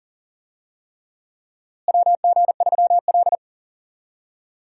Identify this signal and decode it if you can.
Morse code, heard as a keyed tone, about 38 wpm, 700 Hz. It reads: WG3L
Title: USA Call Signs - 4 Characters - Repeated 3x - 40wpm